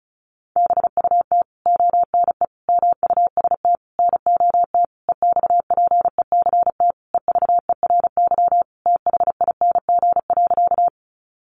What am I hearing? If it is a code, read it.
Morse code, heard as a keyed tone, about 34 wpm, 700 Hz. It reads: BUT ONE MUST NOT EXPECT EVERYTHING.
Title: Sentences from Top 1100 Words 35wpm